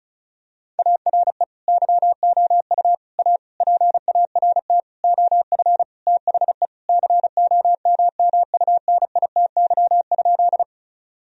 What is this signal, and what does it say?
Morse code, heard as a keyed tone, about 36 wpm, 700 Hz. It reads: ARE YOU A PART OF THE COMMUNITY?
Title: Sentences from Top 1000 Words - Review Part 4 - 35wpm